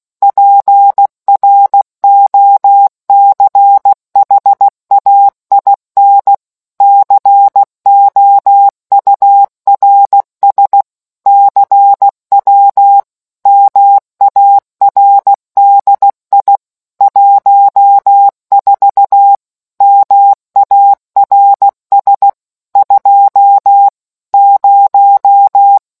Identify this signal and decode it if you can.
Morse code, heard as a keyed tone, about 16 wpm, 800 Hz. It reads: PROCHAIN COURS CW MARDI 14 MARS 20